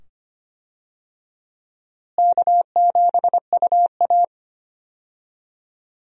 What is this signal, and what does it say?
Morse code, heard as a keyed tone, about 25 wpm, 700 Hz. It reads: K7UA
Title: USA Call Signs - 4 Characters - Repeated 2x - 25wpm